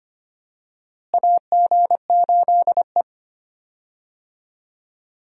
Morse code, heard as a keyed tone, about 25 wpm, 700 Hz. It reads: AG8E
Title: USA Call Signs - 4 Characters - Repeated 2x - 25wpm